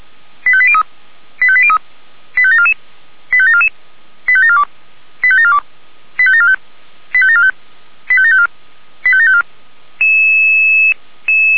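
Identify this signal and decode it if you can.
Five-tone selective call (ZVEI), heard as quick runs of five tones, tones of about 70 ms. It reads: 85803 85803 85744 85744 85742 85742 85745 85745 85754 85754
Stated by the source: ZVEI five-tone call 85726 received by SDR.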